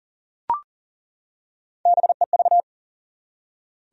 Morse code, heard as a keyed tone, about 40 wpm, 700 Hz. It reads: BEV
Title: USA Common Names - Max 3 Letters - 40wpm